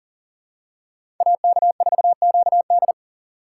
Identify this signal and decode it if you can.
Morse code, heard as a keyed tone, about 40 wpm, 700 Hz. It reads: AK4QD
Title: USA Call Signs - 5 Characters - 2x Character Spacing - 40wpm